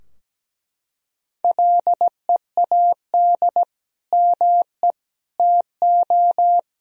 Morse code, heard as a keyed tone, about 17 wpm, 700 Hz. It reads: LEAD ME TO
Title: Sentences from Top 1000 Words - Review Part 2 - 17wpm